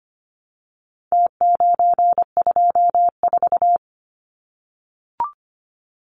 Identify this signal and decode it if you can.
Morse code, heard as a keyed tone, about 25 wpm, 700 Hz. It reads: T924
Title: Numbers - Contest Style - 4-Digit 25wpm